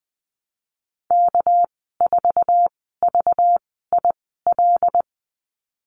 Morse code, heard as a keyed tone, about 20 wpm, spaced as for 14 wpm, 700 Hz. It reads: K4VIL
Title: 245 Lesson 39g - Letter z - Call Signs 20wpm